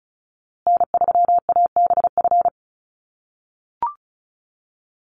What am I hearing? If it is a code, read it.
Morse code, heard as a keyed tone, about 34 wpm, 700 Hz. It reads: N3ABF
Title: USA Call Signs Repeated 2x 35wpm